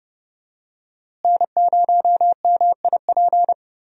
Morse code, heard as a keyed tone, about 29 wpm, 700 Hz. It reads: N0MIP
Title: USA Call Signs - 5 Characters - Repeated 2x - 30wpm